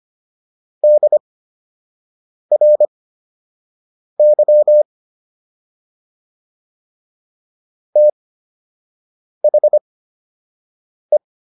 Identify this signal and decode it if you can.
Morse code, heard as a keyed tone, about 25 wpm, spaced as for 6 wpm, 600 Hz. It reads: DRY THE